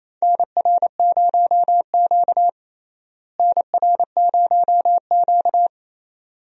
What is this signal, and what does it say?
Morse code, heard as a keyed tone, about 28 wpm, 700 Hz. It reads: NR0Q NR0Q
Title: USA Call Signs - 4 Characters - Repeated 2x - 28wpm